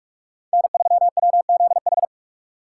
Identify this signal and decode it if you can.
Morse code, heard as a keyed tone, about 46 wpm, 700 Hz. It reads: N3WZH